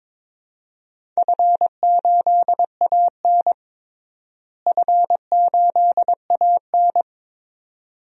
Morse code, heard as a keyed tone, about 21 wpm, 700 Hz. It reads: F8AN F8AN
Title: DXCC-100 Call Signs - 4 Characters - Repeated 2x - 22wpm